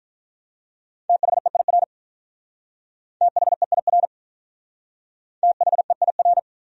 Morse code, heard as a keyed tone, about 41 wpm, 700 Hz. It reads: THEIR THEIR THEIR
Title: Speed-Racing Top 200 Words 35wpm